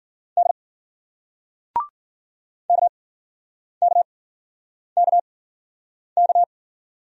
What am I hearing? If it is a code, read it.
Morse code, heard as a keyed tone, about 51 wpm, spaced as for 9 wpm, 700 Hz. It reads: D KKKK